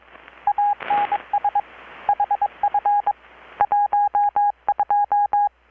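Morse code, heard as a keyed tone, about 24 wpm, 820 Hz. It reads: ANS HF 12
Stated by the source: Morse from KPH's Night of Nights XX maritime radiotelegraphy event, received in Chiapas, Mexico about 3,800 km away on a Kenwood R-600 with a grounded 10 m long-wire antenna.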